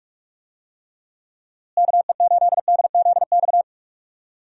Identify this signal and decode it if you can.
Morse code, heard as a keyed tone, about 46 wpm, 700 Hz. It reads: KE8DZX
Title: USA Call Signs - 6 Characters - Repeated 3x - 45wpm